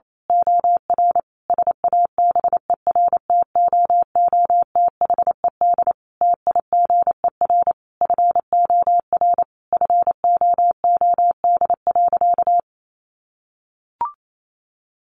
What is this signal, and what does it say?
Morse code, heard as a keyed tone, about 28 wpm, 700 Hz. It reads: OR SABERTOOTHED TIGER FOR FOOD.
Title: A Sound Of Thunder 28wpm